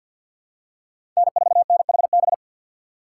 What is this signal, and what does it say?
Morse code, heard as a keyed tone, about 52 wpm, 700 Hz. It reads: N4NHB
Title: USA Call Signs - 5 Characters - 6x Character Spacing - 50wpm